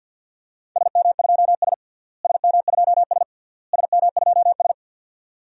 Morse code, heard as a keyed tone, about 48 wpm, 700 Hz. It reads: SM2S SM2S SM2S
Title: DXCC-100 Call Signs - 4 Characters - Repeated 3x - 50wpm